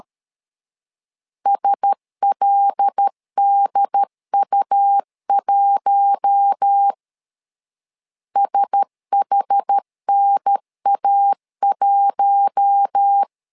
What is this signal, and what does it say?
Morse code, heard as a keyed tone, about 13 wpm, 800 Hz. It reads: SLDU1 SHNA1
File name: LEZIONI / 5 LBVC / 5 L 12wpm 1913